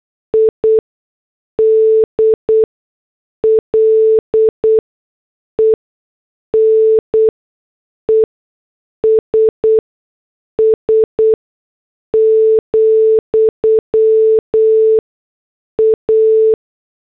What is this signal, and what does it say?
Morse code, heard as a keyed tone, about 8 wpm, spaced as for 6 wpm, 440 Hz. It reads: IDLENESS, A